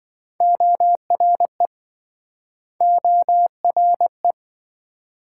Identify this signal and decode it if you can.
Morse code, heard as a keyed tone, about 27 wpm, 700 Hz. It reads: ORE ORE